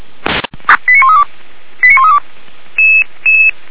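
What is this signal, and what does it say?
Five-tone selective call (ZVEI), heard as quick runs of five tones, tones of about 70 ms. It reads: 89132 89132
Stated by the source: Five-tone ZVEI selective call received on an SDR.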